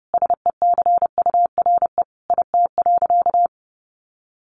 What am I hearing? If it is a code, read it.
Morse code, heard as a keyed tone, about 30 wpm, 700 Hz. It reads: SECURE IT.
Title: Sentences from Top 1200 Words 30wpm